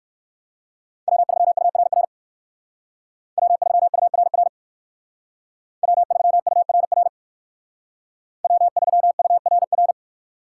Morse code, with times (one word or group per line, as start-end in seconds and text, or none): 1.1-2.0 W3URR
3.4-4.5 W3URR
5.8-7.1 W3URR
8.4-9.9 W3URR